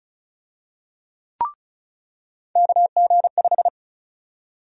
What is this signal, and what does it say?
Morse code, heard as a keyed tone, about 36 wpm, 700 Hz. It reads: KG5
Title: USA Call Sign Prefixes - 35wpm